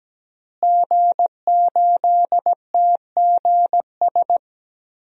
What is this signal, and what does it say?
Morse code, heard as a keyed tone, about 17 wpm, 700 Hz. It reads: G8TGS
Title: DXCC-100 Call Signs - 5 Characters - Repeated 2x - 17wpm